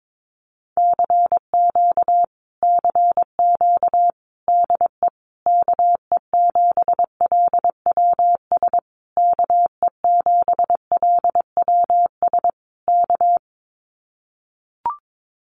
Morse code, heard as a keyed tone, about 22 wpm, 700 Hz. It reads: CQ CQ DE KE7LWS KE7LWS K
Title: Common QSO Phrases Encore 22wpm